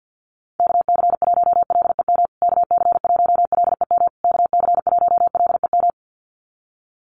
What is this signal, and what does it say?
Morse code, heard as a keyed tone, about 50 wpm, 700 Hz. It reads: KC1LEM KC1LEM KC1LEM
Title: USA Call Signs Repeated 3x 50wpm